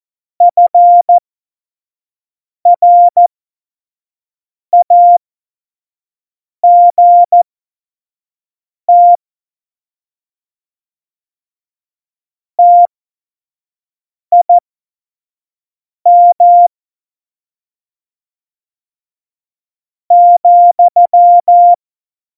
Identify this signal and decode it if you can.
Morse code, heard as a keyed tone, about 14 wpm, spaced as for 4 wpm, 700 Hz. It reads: FRAGT TIM,
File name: Morsekurs / Witze / Witz 16. Juli 2019, 25 F70